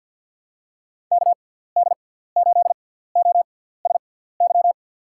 Morse code, heard as a keyed tone, about 53 wpm, spaced as for 17 wpm, 700 Hz. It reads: KD8OSY
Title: USA Call Signs - 6 Characters - 6x Character Spacing - 50wpm